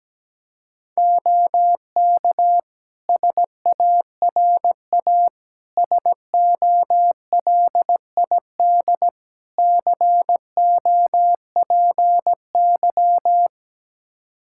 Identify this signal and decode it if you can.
Morse code, heard as a keyed tone, about 17 wpm, 700 Hz. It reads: OK SARA SOLID COPY
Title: Practice QSO - Exchange 3 - More Info - 17wpm